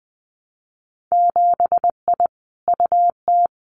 Morse code, heard as a keyed tone, about 20 wpm, 700 Hz. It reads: 7I UT
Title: Winter Field Day Exchanges Encore 20wpm